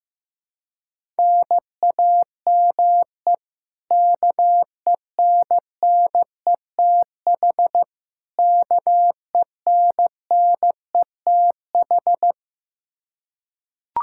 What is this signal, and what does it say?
Morse code, heard as a keyed tone, about 15 wpm, 700 Hz. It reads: NAME KENNETH KENNETH
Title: Practice QSO - Exchange 1 - Beginning - 15wpm